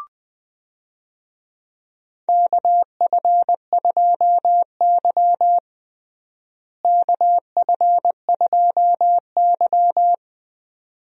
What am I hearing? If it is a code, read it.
Morse code, heard as a keyed tone, about 21 wpm, 700 Hz. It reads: KF2Y KF2Y
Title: USA Call Signs - 4 Characters - Repeated 2x - 20wpm